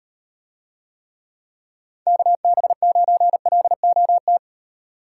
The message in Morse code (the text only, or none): KB9LOT